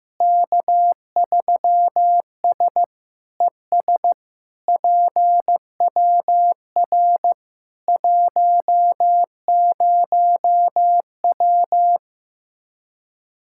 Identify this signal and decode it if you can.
Morse code, heard as a keyed tone, about 15 wpm, 700 Hz. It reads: K3S ES PWR 10W
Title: Practice QSO - Exchange 2 - Info - 15wpm